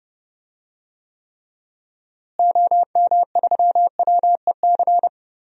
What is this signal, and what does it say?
Morse code, heard as a keyed tone, about 30 wpm, 700 Hz. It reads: OM3WEC
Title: DXCC-100 Call Signs - 6 Characters - Repeated 2x - 30wpm